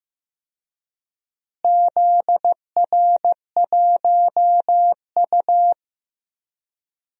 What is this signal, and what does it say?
Morse code, heard as a keyed tone, about 15 wpm, 700 Hz. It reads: ZR1U
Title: DXCC-100 Call Signs - 4 Characters - Repeated 2x - 15wpm